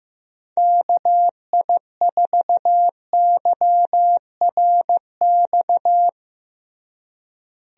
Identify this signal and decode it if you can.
Morse code, heard as a keyed tone, about 15 wpm, 700 Hz. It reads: KI4YRX
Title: USA Call Signs - 6 Characters - Repeated 2x - 15wpm